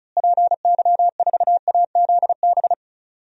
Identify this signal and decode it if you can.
Morse code, heard as a keyed tone, about 34 wpm, 700 Hz. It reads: PY4AZB
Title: DXCC-100 Call Signs - 6 Characters - Repeated 3x - 35wpm